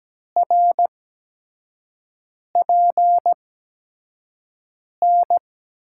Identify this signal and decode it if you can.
Morse code, heard as a keyed tone, about 17 wpm, spaced as for 5 wpm, 700 Hz. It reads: RPN
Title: USA Call Signs - 6 Characters - 8x Character Spacing - 17wpm